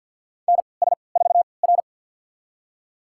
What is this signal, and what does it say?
Morse code, heard as a keyed tone, about 53 wpm, spaced as for 29 wpm, 700 Hz. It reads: NS4R
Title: USA Call Signs - 4 Characters - 3x Character Spacing - 50wpm